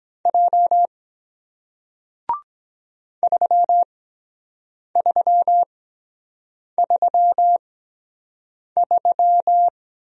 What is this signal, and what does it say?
Morse code, heard as a keyed tone, about 23 wpm, spaced as for 7 wpm, 700 Hz. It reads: J 3333